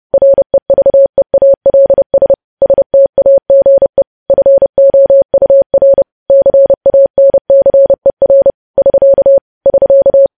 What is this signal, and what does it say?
Morse code, heard as a keyed tone, about 30 wpm, 550 Hz. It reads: REVEALS STAGE FOUR CANCER <SK> <SK>
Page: news headlines in morse code at 30 wpm